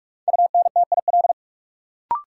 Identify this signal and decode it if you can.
Morse code, heard as a keyed tone, about 46 wpm, 700 Hz. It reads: UNTIL
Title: Sets of 5 Words Encore - New Word Drill - 45wpm